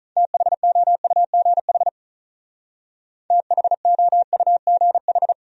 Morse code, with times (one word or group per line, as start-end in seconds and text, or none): 0.2-1.9 THOUGH
3.3-5.3 THOUGH